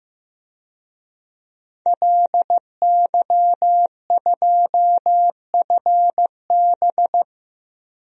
Morse code, heard as a keyed tone, about 15 wpm, 700 Hz. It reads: LY2FB
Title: DXCC-100 Call Signs - 5 Characters - Repeated 2x - 15wpm